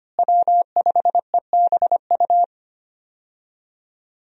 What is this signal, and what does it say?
Morse code, heard as a keyed tone, about 25 wpm, 700 Hz. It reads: W5EBU
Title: USA Call Signs - 5 Characters - Repeated 2x - 25wpm